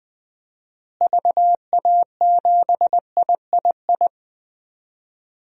Morse code, heard as a keyed tone, about 20 wpm, 700 Hz. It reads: VA7III
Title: DXCC-100 Call Signs - 6 Characters - Repeated 3x - 20wpm